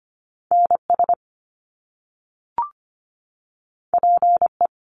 Morse code, heard as a keyed tone, about 25 wpm, 700 Hz. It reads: NS PE
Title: Canadian Provinces and Territories 25wpm